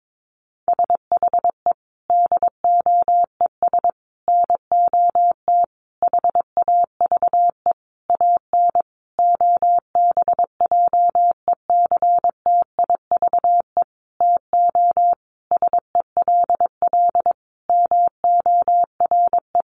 Morse code, heard as a keyed tone, about 22 wpm, 700 Hz. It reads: SHE DOES NOT HAVE AN OBJECTIVE TO SELL MORE
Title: Sentences from Top 1200 Words 22wpm